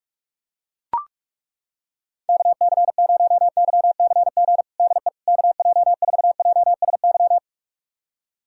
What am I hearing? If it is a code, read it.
Morse code, heard as a keyed tone, about 42 wpm, 700 Hz. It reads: KC0YCG DE KJ4JSY
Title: USA Callsign DE Callsign - 6 Characters - 45wpm